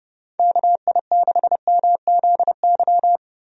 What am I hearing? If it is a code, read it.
Morse code, heard as a keyed tone, about 29 wpm, 700 Hz. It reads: KI6MZY
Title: USA Call Signs - 6 Characters - 30wpm